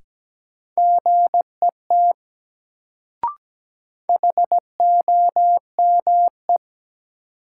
Morse code, heard as a keyed tone, about 17 wpm, 700 Hz. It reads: GET HOME
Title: Sentences from Top 200 Words - New Word Drill - 17wpm